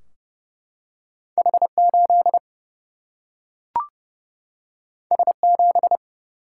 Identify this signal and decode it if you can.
Morse code, heard as a keyed tone, about 30 wpm, 700 Hz. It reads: H 8 S 7
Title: Single-Letter Single-Number 30wpm